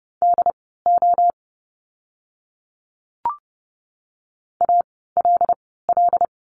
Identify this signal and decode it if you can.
Morse code, heard as a keyed tone, about 30 wpm, spaced as for 17 wpm, 700 Hz. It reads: DO ALL
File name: Lesson 17f - Letter u - Words 30wpm